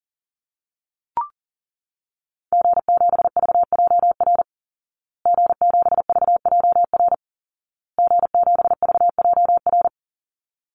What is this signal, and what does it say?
Morse code, heard as a keyed tone, about 40 wpm, 700 Hz. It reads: G7VJR G7VJR G7VJR
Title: Call Signs Repeated 2x - 40wpm